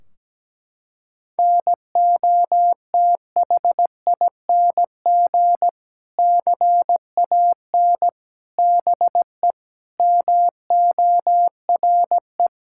Morse code, heard as a keyed tone, about 17 wpm, 700 Hz. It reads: NOTHING CAN BE MORE
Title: Sentences from Top 1600 Words 17wpm